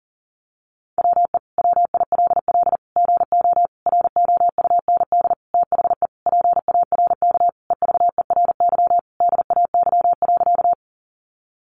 Morse code, heard as a keyed tone, about 40 wpm, 700 Hz. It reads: WE WILL GO ROUND THE PARK EVERY DAY.
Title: Sentences from Top 1300 Words 40wpm